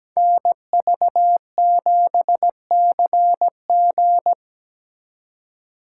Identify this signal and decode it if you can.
Morse code, heard as a keyed tone, about 20 wpm, 700 Hz. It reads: NV7CG
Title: Speed Racing - USA 2x2 Call Signs 17wpm